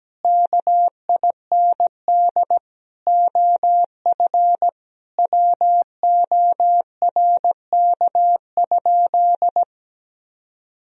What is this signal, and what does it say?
Morse code, heard as a keyed tone, about 17 wpm, 700 Hz. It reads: KIND OF WORK?
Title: Sentences from Top 1000 Words - Review Part 3 - 17wpm